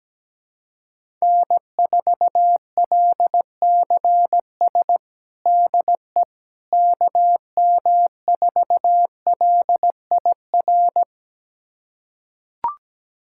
Morse code, heard as a keyed tone, about 17 wpm, 700 Hz. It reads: N4LCS DE KM4LIR
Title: USA Callsign DE Callsign 17wpm